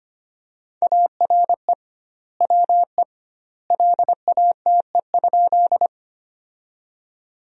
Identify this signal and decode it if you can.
Morse code, heard as a keyed tone, about 25 wpm, 700 Hz. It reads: ARE WE LATE?
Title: 2x Word-Spacing - Sentences from Top 300 Words - 25wpm